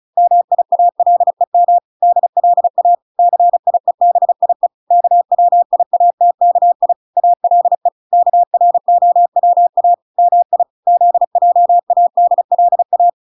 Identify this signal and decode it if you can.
Morse code, heard as a keyed tone, about 34 wpm, 700 Hz. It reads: MIALEM DLA CIEBIE KWIATKI ALE KROWA MI ZJADLA